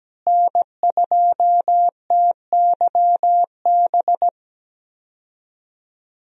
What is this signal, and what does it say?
Morse code, heard as a keyed tone, about 17 wpm, 700 Hz. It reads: N2TYB
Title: USA Call Signs - 5 Characters - Repeated 3x - 17wpm